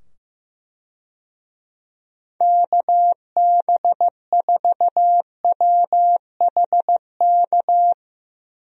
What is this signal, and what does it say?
Morse code, heard as a keyed tone, about 15 wpm, 700 Hz. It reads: KB4WHK
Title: USA Call Signs - 6 Characters - 15wpm